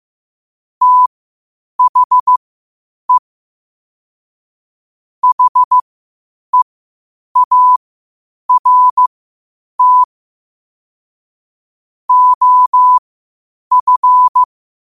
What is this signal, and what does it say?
Morse code, heard as a keyed tone, about 15 wpm, spaced as for 8 wpm, 1000 Hz. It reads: THE HEART OF
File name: qotdfiles / 8wpm